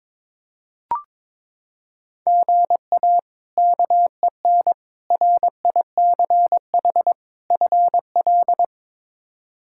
Morse code, heard as a keyed tone, about 22 wpm, 700 Hz. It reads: GA KEN RICH FL
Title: K1USN SST QSO Practice - 22wpm